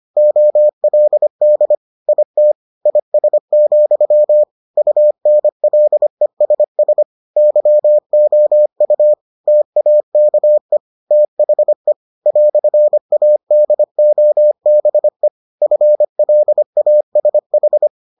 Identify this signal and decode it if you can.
Morse code, heard as a keyed tone, about 25 wpm, 600 Hz. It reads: OLD IT IS, UNLESS YOU TAKE THE "ADOBE FLASH